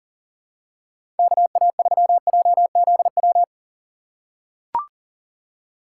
Morse code, heard as a keyed tone, about 40 wpm, 700 Hz. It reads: KA3JZW
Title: USA Call Signs - 2x Character Spacing - 40wpm